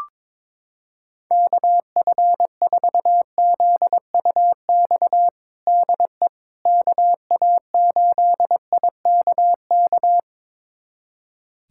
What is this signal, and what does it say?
Morse code, heard as a keyed tone, about 22 wpm, 700 Hz. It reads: KF4ZUX DE KA8IKK
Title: USA Callsign DE Callsign - 6 Characters - 22wpm